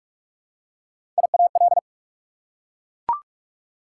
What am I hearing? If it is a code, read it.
Morse code, heard as a keyed tone, about 44 wpm, 690 Hz. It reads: IAL
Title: Prefixes and Suffixes 45wpm